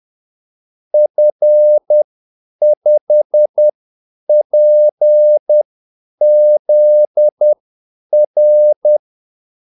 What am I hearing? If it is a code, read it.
Morse code, heard as a keyed tone, about 10 wpm, spaced as for 8 wpm, 600 Hz. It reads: F5PZR
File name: Callsigns-10-8wpm